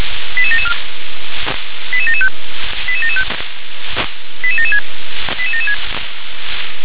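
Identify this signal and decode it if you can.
Five-tone selective call (ZVEI), heard as quick runs of five tones, tones of about 70 ms. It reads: 88774 88775 88775 88776 88776
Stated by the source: ZVEI five-tone selective call 88773 received by SDR.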